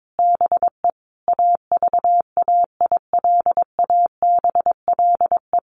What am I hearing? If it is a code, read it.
Morse code, heard as a keyed tone, about 22 wpm, 700 Hz. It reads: BE AVAILABLE
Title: Sentences from Top 1400 Words 22wpm